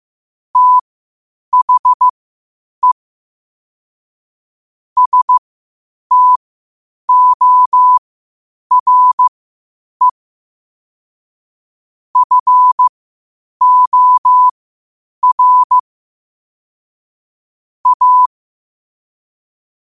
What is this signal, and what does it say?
Morse code, heard as a keyed tone, about 15 wpm, spaced as for 8 wpm, 1000 Hz. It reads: THE STORE FOR A
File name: qotdfiles / 8wpm